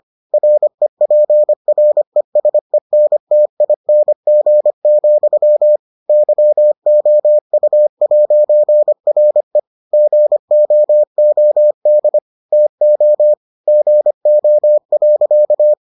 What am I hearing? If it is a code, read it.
Morse code, heard as a keyed tone, about 25 wpm, 600 Hz. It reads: REPRESENTING, YOU'RE GOOD TO GO.